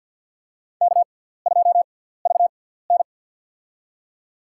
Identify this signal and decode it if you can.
Morse code, heard as a keyed tone, about 52 wpm, spaced as for 17 wpm, 690 Hz. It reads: K2VN
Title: USA Call Signs - 6x Character Spacing - 50wpm